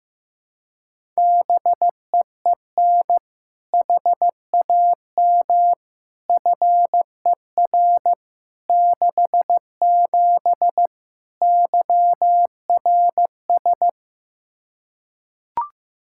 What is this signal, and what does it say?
Morse code, heard as a keyed tone, about 15 wpm, 700 Hz. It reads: BEEN HAM FER 67 YRS
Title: Practice QSO - Exchange 3 - More Info - 15wpm